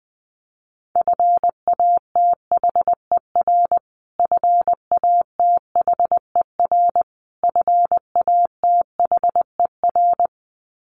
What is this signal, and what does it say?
Morse code, heard as a keyed tone, about 20 wpm, 700 Hz. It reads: FATHER FATHER FATHER
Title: Top 500 Words Repeated 20wpm